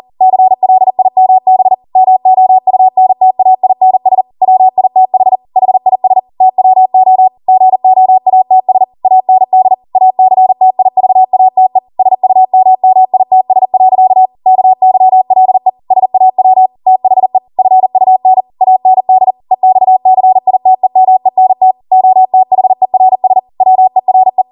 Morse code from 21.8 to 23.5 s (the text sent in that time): OTHERS